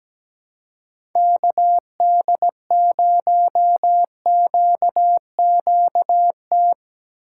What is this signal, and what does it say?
Morse code, heard as a keyed tone, about 19 wpm, 700 Hz. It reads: KD0QQT
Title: Speed Racing - USA 2x3 Call Signs 17wpm